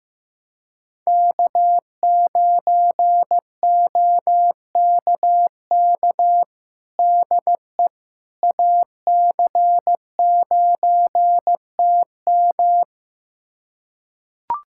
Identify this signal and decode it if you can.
Morse code, heard as a keyed tone, about 16 wpm, 700 Hz. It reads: K9OKK DE AC9TM
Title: USA Callsign DE Callsign - 5 Characters - 15wpm